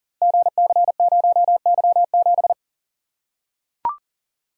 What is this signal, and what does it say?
Morse code, heard as a keyed tone, about 40 wpm, 700 Hz. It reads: GC0Y7
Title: Code Groups - 5 Letter:Numbers - 40wpm